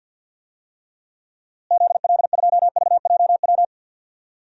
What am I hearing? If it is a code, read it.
Morse code, heard as a keyed tone, about 47 wpm, 700 Hz. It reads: ZL2VJW
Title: DXCC-100 Call Signs - 6 Characters - Repeated 3x - 50wpm